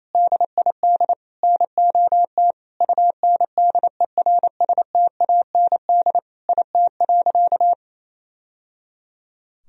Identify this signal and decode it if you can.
Morse code, heard as a keyed tone, about 27 wpm, 700 Hz. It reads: DID NOT UNDERSTAND IT.
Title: Sentences from Top 1000 Words - Review Part 3 - 28wpm